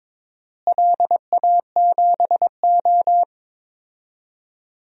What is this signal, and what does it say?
Morse code, heard as a keyed tone, about 22 wpm, 700 Hz. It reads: LA7O
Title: DXCC-100 Call Signs - 4 Characters - Repeated 3x - 22wpm